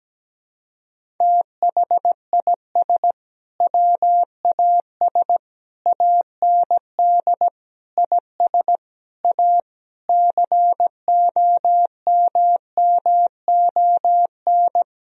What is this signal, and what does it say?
Morse code, heard as a keyed tone, about 17 wpm, 700 Hz. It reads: THIS WAS AND IS A COMMON
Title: Sentences from Top 1000 Words - Review Part 4 - 17wpm